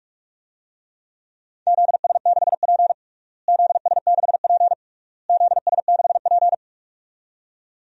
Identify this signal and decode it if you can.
Morse code, heard as a keyed tone, about 46 wpm, 690 Hz. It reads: ZS6P ZS6P ZS6P
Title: DXCC-100 Call Signs - 4 Characters - Repeated 3x - 45wpm